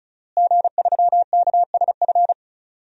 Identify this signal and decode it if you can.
Morse code, heard as a keyed tone, about 36 wpm, 700 Hz. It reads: G3KSF
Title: DXCC-100 Call Signs - 5 Characters - Repeated 3x - 35wpm